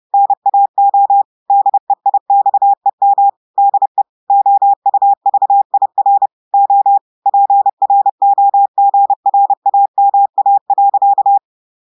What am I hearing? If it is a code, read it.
Morse code, heard as a keyed tone, about 30 wpm, 800 Hz. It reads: NAO DEIXEM DE OUVIR O PROGRAMA.
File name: audios / portugais / portugais texte en clair 2 vitesse 30 mots minutes